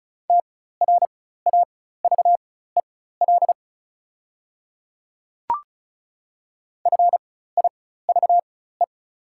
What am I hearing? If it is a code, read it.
Morse code, heard as a keyed tone, about 36 wpm, spaced as for 16 wpm, 700 Hz. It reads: TRAVEL FIVE
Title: 4x Character Spacing - Top 500 Words - 35wpm